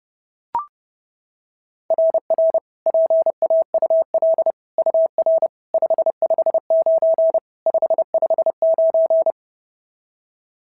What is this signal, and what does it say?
Morse code, heard as a keyed tone, about 30 wpm, 690 Hz. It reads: RR PAUL UR 559 559
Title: SOTA QSO Practice 30wpm